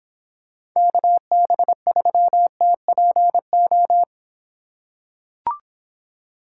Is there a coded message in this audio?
Yes: Morse code from 0.8 to 4.0 s, heard as a keyed tone, about 24 wpm, 700 Hz. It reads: KB3TPO